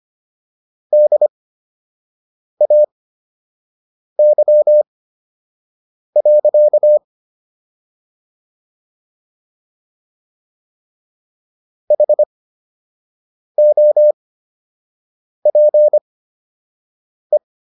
Morse code, heard as a keyed tone, about 25 wpm, spaced as for 6 wpm, 600 Hz. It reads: DAY. HOPE